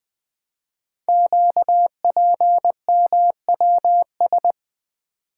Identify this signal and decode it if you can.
Morse code, heard as a keyed tone, about 20 wpm, 700 Hz. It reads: QPMWS
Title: Code Groups - 5 Letters - 20wpm